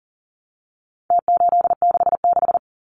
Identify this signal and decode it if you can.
Morse code, heard as a keyed tone, about 40 wpm, 700 Hz. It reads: T866
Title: Numbers - Contest Style - 4-Digit 40wpm